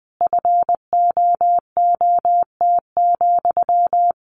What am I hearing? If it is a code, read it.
Morse code, heard as a keyed tone, about 20 wpm, 700 Hz. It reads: FOOT,
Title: A Sound Of Thunder 20wpm